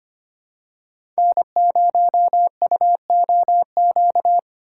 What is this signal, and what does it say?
Morse code, heard as a keyed tone, about 25 wpm, 700 Hz. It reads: N0UOQ
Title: USA Call Signs - 5 Characters - 2x Character Spacing - 25wpm